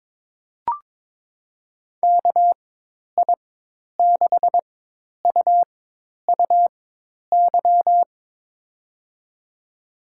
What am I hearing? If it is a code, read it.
Morse code, heard as a keyed tone, about 22 wpm, spaced as for 10 wpm, 700 Hz. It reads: KI6UUY
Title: USA Call Signs - 6 Characters - 4x Character Spacing - 22wpm